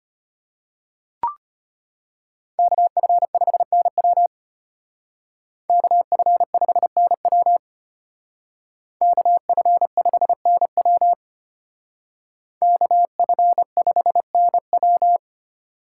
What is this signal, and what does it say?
Morse code, heard as a keyed tone, about 32 wpm, 700 Hz. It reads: KF5NW KF5NW KF5NW KF5NW